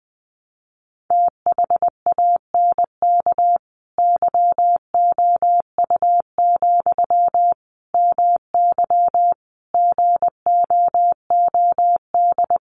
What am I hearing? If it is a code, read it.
Morse code, heard as a keyed tone, about 20 wpm, 700 Hz. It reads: THANK YOU, MY GOOD
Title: Sentences from Top 600 Words 20wpm